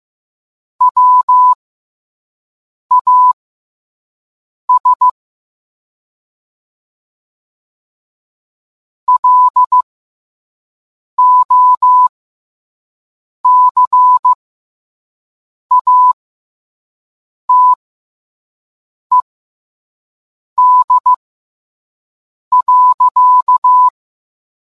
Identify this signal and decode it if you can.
Morse code, heard as a keyed tone, about 15 wpm, spaced as for 5 wpm, 1000 Hz. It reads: WAS LOCATED.